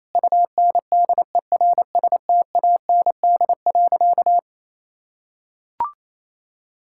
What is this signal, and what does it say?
Morse code, heard as a keyed tone, about 28 wpm, 700 Hz. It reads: UNDERSTAND.
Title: Sentences from Top 1000 Words - Review Part 5 - 28wpm